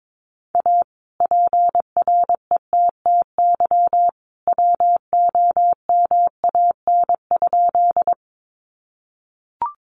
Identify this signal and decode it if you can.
Morse code, heard as a keyed tone, about 22 wpm, 700 Hz. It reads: A PRETTY WOMAN?
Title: Sentences from Top 1100 Words 22wpm